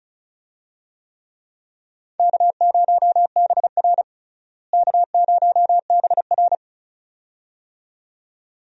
Morse code, heard as a keyed tone, about 35 wpm, 700 Hz. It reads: K0BR K0BR
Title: USA Call Signs - 4 Characters - Repeated 2x - 35wpm